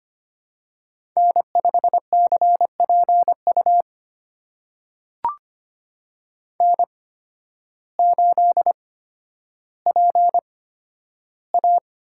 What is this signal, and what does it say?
Morse code, heard as a keyed tone, about 25 wpm, spaced as for 12 wpm, 700 Hz. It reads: N5CPU N8PA
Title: USA Call Signs - 5 Characters - 8x Character Spacing - 25wpm